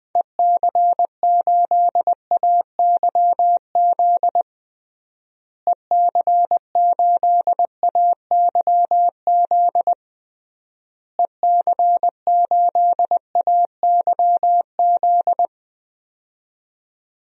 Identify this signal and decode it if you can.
Morse code, heard as a keyed tone, about 20 wpm, 700 Hz. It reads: EC8AYZ EC8AYZ EC8AYZ
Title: DXCC-100 Call Signs - 6 Characters - Repeated 3x - 20wpm